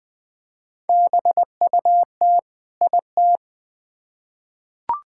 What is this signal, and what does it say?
Morse code, heard as a keyed tone, about 20 wpm, 700 Hz. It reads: BUT IT
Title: Build A Phrase - 2-Word Phrases - 20wpm